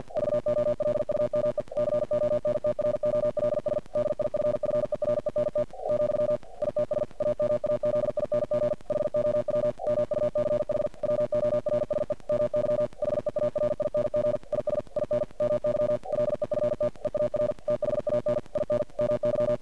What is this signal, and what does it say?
Morse code, heard as a keyed tone, about 50 wpm, 600 Hz. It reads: VOLUME CONTROLS DIFFERENT , ITS AMAZING HOW MUCH WORSE MY HEARING IS IN MY LEFT EAR THAN IN MY
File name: archived / CRSnet